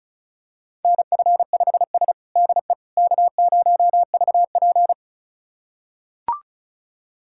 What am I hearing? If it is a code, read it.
Morse code, heard as a keyed tone, about 36 wpm, 700 Hz. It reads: NF5S DE K0VP
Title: USA Callsign DE Callsign - 4 Characters - 35wpm